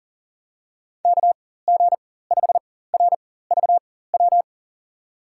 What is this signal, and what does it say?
Morse code, heard as a keyed tone, about 40 wpm, spaced as for 19 wpm, 700 Hz. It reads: KG5RVW
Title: USA Call Signs - 4x Character Spacing - 40wpm